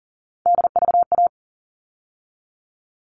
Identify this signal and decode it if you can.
Morse code, heard as a keyed tone, about 40 wpm, 700 Hz. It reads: DVA
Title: All Cut Numbers - 3-Digit 40wpm